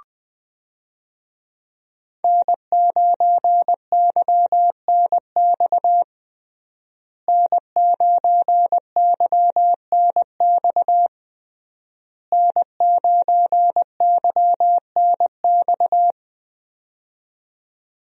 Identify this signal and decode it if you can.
Morse code, heard as a keyed tone, about 21 wpm, 700 Hz. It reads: N9YNX N9YNX N9YNX
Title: USA Call Signs - 5 Characters - Repeated 3x - 20wpm